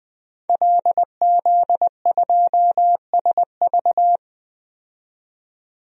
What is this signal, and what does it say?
Morse code, heard as a keyed tone, about 20 wpm, 700 Hz. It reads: LZ2SV
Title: DXCC-100 Call Signs - 5 Characters - 20wpm